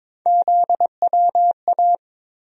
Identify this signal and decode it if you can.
Morse code, heard as a keyed tone, about 22 wpm, 700 Hz. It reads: ZWA
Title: USA Call Sign Postfixes - 3 Letters 22wpm